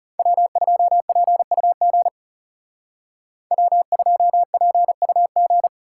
Morse code, heard as a keyed tone, about 36 wpm, 700 Hz. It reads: W2PUG W2PUG
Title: Speed Racing - USA Call Signs 30wpm